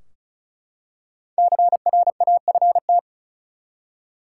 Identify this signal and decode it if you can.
Morse code, heard as a keyed tone, about 34 wpm, 700 Hz. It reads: CRAFT
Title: Five Letter Words - 35wpm